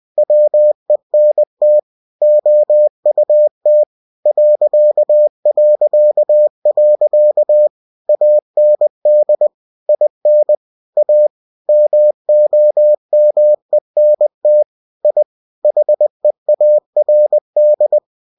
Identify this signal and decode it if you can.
Morse code, heard as a keyed tone, about 20 wpm, 600 Hz. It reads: WENT OUT ... AND IN A MOMENT I HEARD
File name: cw-books / winnie-the-pooh / winnie-20wpm-page-0111